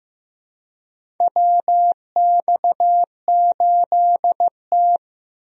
Morse code, heard as a keyed tone, about 15 wpm, 700 Hz. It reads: WX8T
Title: USA Call Signs - 4 Characters - Repeated 3x - 15wpm